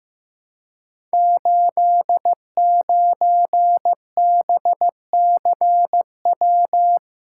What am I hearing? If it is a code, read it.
Morse code, heard as a keyed tone, about 15 wpm, 700 Hz. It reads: 89BCW
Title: Code Groups - 5 Letter:Numbers - 15wpm